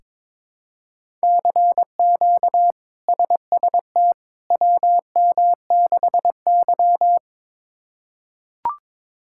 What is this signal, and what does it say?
Morse code, heard as a keyed tone, about 22 wpm, 700 Hz. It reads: CQ SST WM6Y
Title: K1USN SST QSO Practice - 22wpm